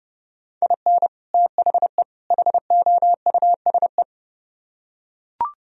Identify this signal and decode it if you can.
Morse code, heard as a keyed tone, about 31 wpm, 700 Hz. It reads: IN THE HOUSE